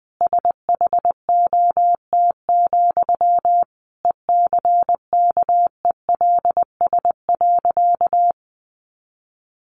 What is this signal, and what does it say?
Morse code, heard as a keyed tone, about 20 wpm, 700 Hz. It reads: SHOT, ECKELS.
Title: A Sound Of Thunder 20wpm